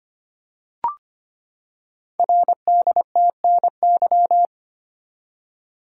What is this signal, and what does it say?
Morse code, heard as a keyed tone, about 25 wpm, 700 Hz. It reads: RDTNY
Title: Code Groups - 5 Letters - 25wpm